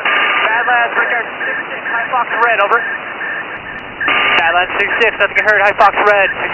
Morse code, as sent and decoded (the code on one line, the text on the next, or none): none